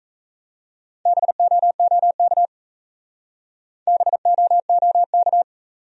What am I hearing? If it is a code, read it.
Morse code, heard as a keyed tone, about 34 wpm, 700 Hz. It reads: BOOK BOOK
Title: Speed-Racing Top 300 Words 28wpm